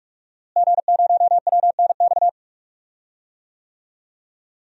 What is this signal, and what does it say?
Morse code, heard as a keyed tone, about 47 wpm, 700 Hz. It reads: G0WNX
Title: DXCC-100 Call Signs - 5 Characters - Repeated 3x - 45wpm